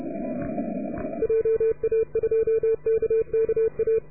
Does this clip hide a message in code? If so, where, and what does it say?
Yes: Morse code from 1.2 to 4.0 s, heard as a keyed tone, about 31 wpm, 450 Hz. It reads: JA2KKA